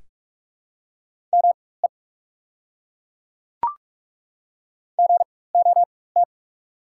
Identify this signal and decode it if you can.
Morse code, heard as a keyed tone, about 43 wpm, spaced as for 21 wpm, 690 Hz. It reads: ME GOT
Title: 4x Character Spacing - Top 300 Words - 45wpm